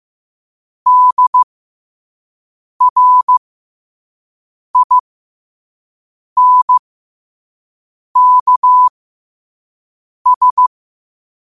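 Morse code, heard as a keyed tone, about 15 wpm, spaced as for 5 wpm, 1000 Hz. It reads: DRINKS
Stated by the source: Morse code practice at 5 wpm.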